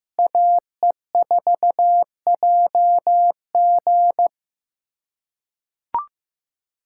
Morse code, heard as a keyed tone, about 15 wpm, 700 Hz. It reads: AE4JG
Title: USA Callsign DE Callsign - 5 Characters - 15wpm